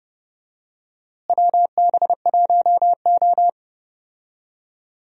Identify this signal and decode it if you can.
Morse code, heard as a keyed tone, about 30 wpm, 700 Hz. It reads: WB1O
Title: USA Call Signs - 4 Characters - Repeated 3x - 30wpm